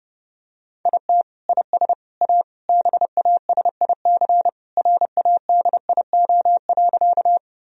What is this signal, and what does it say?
Morse code, heard as a keyed tone, about 30 wpm, 700 Hz. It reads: IT IS A BASIC RADIO.
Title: Sentences from Top 1000 Words - Review Part 3 - 30wpm